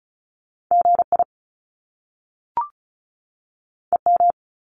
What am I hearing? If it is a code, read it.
Morse code, heard as a keyed tone, about 36 wpm, 700 Hz. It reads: GI EM
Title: Top Two Letter Combinations in Words 35wpm